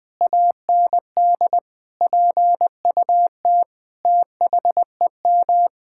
Morse code, heard as a keyed tone, about 20 wpm, 700 Hz. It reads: AND PUT THEM
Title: Sentences from Top 1000 Words - Review Part 5 - 20wpm